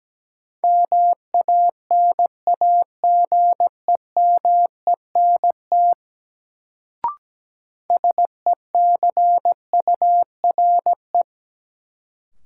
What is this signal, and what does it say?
Morse code, heard as a keyed tone, about 17 wpm, 700 Hz. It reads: MANAGEMENT SECURE
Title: Sentences from Top 1200 Words - New Word Drill Alt - 17wpm